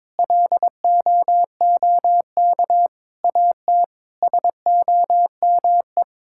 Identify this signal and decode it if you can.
Morse code, heard as a keyed tone, about 22 wpm, 700 Hz. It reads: LOOK AT SOME
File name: practice / 22wpm / Sets of 7 Words - 22wpm